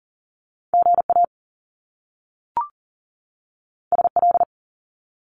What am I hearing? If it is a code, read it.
Morse code, heard as a keyed tone, about 41 wpm, 700 Hz. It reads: GA SL
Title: Top Two Letter Combinations in Words 40wpm